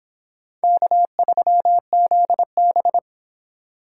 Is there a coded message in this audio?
Yes: Morse code from 0.6 to 3.0 s, heard as a keyed tone, about 32 wpm, 700 Hz. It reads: K3ZB